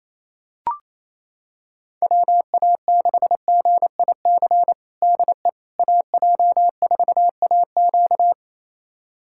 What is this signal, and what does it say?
Morse code, heard as a keyed tone, about 28 wpm, 700 Hz. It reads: WA6GIC DE AJ4AQ
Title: USA Callsign DE Callsign 28wpm